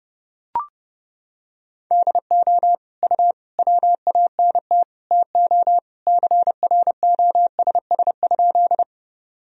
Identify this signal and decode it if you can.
Morse code, heard as a keyed tone, about 30 wpm, 700 Hz. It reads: DO U WANT TO CROSS?
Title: Sentences from Top 300 Words and Abbreviations 30wpm